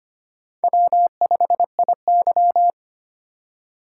Morse code, heard as a keyed tone, about 25 wpm, 700 Hz. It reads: W5IY
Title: USA Call Signs - 4 Characters - 3x Character Spacing - 25wpm